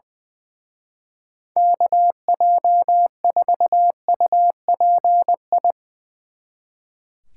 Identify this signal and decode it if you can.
Morse code, heard as a keyed tone, about 22 wpm, 700 Hz. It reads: KJ4UPI
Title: Speed Racing - USA 2x3 Call Signs 20wpm